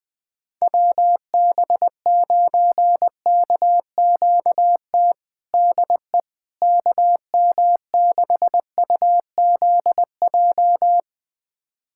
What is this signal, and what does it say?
Morse code, heard as a keyed tone, about 20 wpm, 700 Hz. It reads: WB9KQT DE KM6UZJ
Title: USA Callsign DE Callsign - 6 Characters - 20wpm